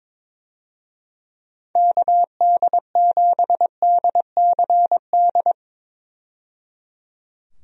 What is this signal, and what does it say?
Morse code, heard as a keyed tone, about 23 wpm, 700 Hz. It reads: KD7DCD
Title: USA Call Signs - 6 Characters - 22wpm